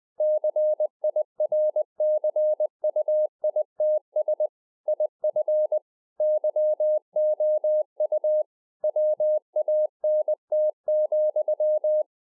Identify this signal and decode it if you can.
Morse code, heard as a keyed tone, about 20 wpm, 610 Hz. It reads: CIRCUITS IF YOU WANT,
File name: 2011-04-mp3 / A SMALL TRANCEIVER 3 20WPM